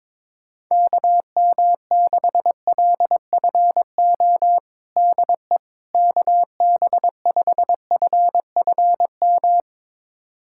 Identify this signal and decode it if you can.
Morse code, heard as a keyed tone, about 22 wpm, 700 Hz. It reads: KM6LFO DE KB5FFM
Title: USA Callsign DE Callsign - 6 Characters - 22wpm